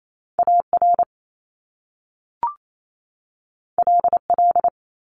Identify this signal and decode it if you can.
Morse code, heard as a keyed tone, about 28 wpm, 700 Hz. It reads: AR LL
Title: Top Two Letter Combinations in Words 28wpm